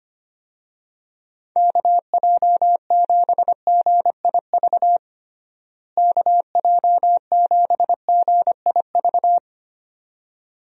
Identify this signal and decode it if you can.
Morse code, heard as a keyed tone, about 25 wpm, 700 Hz. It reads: KJ7GIV KJ7GIV
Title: USA Call Signs - 6 Characters - Repeated 2x - 25wpm